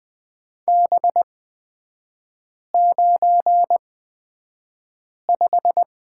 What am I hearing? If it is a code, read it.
Morse code, heard as a keyed tone, about 20 wpm, spaced as for 5 wpm, 700 Hz. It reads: B95
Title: 229 Lesson 37e - Letter x - Three Characters 20wpm